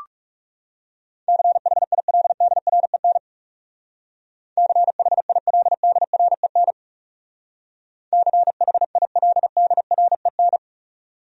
Morse code, heard as a keyed tone, about 38 wpm, 700 Hz. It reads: CHILDREN CHILDREN CHILDREN